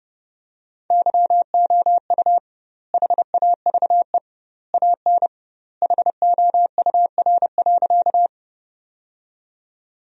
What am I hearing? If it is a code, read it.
Morse code, heard as a keyed tone, about 30 wpm, 700 Hz. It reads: YOU HAVE AN HOUR.
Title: 2x Word-Spacing - Sentences from Top 400 Words - 30wpm